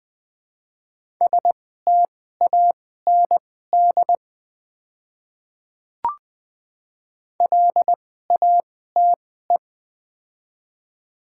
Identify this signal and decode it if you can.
Morse code, heard as a keyed tone, about 20 wpm, spaced as for 14 wpm, 700 Hz. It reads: STAND LATE
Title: 2x Character Spacing - Top 500 Words - 20wpm